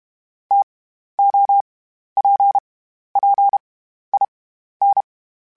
Morse code, heard as a keyed tone, about 32 wpm, 800 Hz. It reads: T O P P I N
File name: Variable Speed / Variable Speed-mp3s / TO sp @30